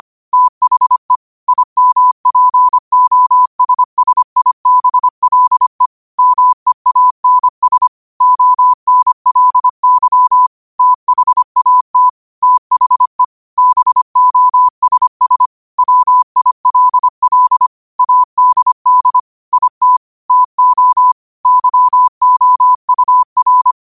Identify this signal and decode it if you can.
Morse code, heard as a keyed tone, about 25 wpm, 1000 Hz. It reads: THE IMPOSSIBLE MEANS ONLY THAT THE BOSS WILL ADD IT TO YOUR